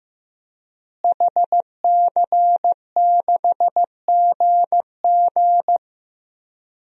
Morse code, heard as a keyed tone, about 15 wpm, 700 Hz. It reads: HC6GG
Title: DXCC-100 Call Signs - 5 Characters - Repeated 2x - 15wpm